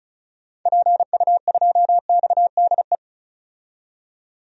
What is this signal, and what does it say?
Morse code, heard as a keyed tone, about 36 wpm, 700 Hz. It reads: PU2XDE
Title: DXCC-100 Call Signs - 6 Characters - 35wpm